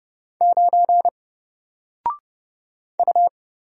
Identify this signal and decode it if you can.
Morse code, heard as a keyed tone, about 31 wpm, 700 Hz. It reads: 9 U
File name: Single-Letter-Number-30wpm